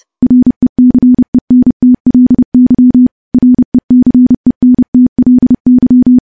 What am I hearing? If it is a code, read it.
Morse code, heard as a keyed tone, about 30 wpm, 250 Hz. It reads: RECENTLY RECENTLY
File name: morse-training-30WPM